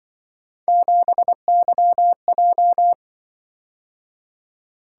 Morse code, heard as a keyed tone, about 25 wpm, 700 Hz. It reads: 7YJ
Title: Speed-Racing - Three Letter-Number Combinations - 20wpm